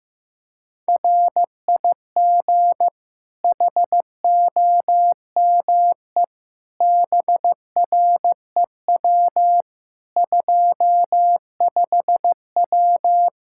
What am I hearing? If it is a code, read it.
Morse code, heard as a keyed tone, about 15 wpm, 700 Hz. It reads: RIG HOME BREW 25W
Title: K1USN SST QSO Practice 15wpm